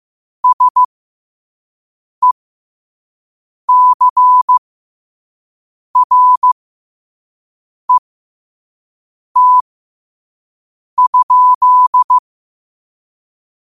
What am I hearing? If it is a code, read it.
Morse code, heard as a keyed tone, about 15 wpm, spaced as for 5 wpm, 1000 Hz. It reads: SECRET?
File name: qotdfiles / 5wpm